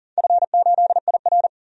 Morse code, heard as a keyed tone, about 40 wpm, 700 Hz. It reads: F8IR
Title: DXCC-100 Call Signs - 4 Characters - 40wpm